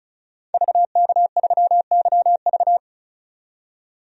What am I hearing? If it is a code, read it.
Morse code, heard as a keyed tone, about 34 wpm, 700 Hz. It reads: VK3YV
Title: DXCC-100 Call Signs - 5 Characters - 35wpm